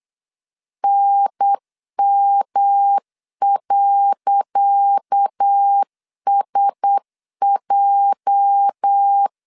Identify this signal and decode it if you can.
Morse code, heard as a keyed tone, about 8 wpm, 800 Hz. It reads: NM.SJ
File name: LEZIONI / 10 SIGNS / 10C DOT 8wpm 3649